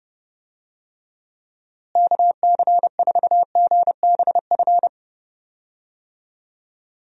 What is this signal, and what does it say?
Morse code, heard as a keyed tone, about 29 wpm, 700 Hz. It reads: KC4GBF
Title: USA Call Signs - 6 Characters - Repeated 3x - 30wpm